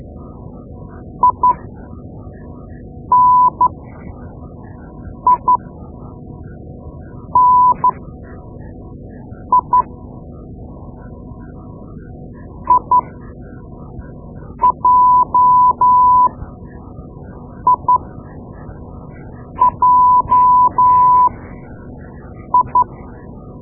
Morse code, heard as a keyed tone, about 12 wpm, spaced as for 5 wpm, 1000 Hz. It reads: ININI IJIJI
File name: Morse Lessons / Morse Lessons / Tape-2B